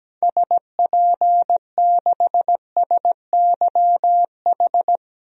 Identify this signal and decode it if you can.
Morse code, heard as a keyed tone, about 17 wpm, 700 Hz. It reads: SP6SYH
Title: DXCC-100 Call Signs - 6 Characters - 17wpm